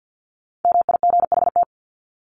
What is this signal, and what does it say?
Morse code, heard as a keyed tone, about 48 wpm, 700 Hz. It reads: MIGHT
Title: Top 300 Words 50wpm